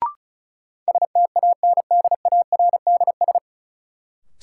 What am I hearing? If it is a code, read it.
Morse code, heard as a keyed tone, about 34 wpm, 700 Hz. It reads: STANDARDS
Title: Sentences from Top 1000 Words - Review Part 6 - New Word Drill - 35wpm